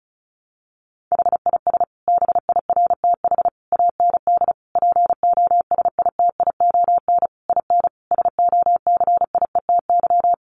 Morse code, heard as a keyed tone, about 34 wpm, 700 Hz. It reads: HIS BIRTH AND POSITION IN SOCIETY
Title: Sentences from Top 900 Words 35wpm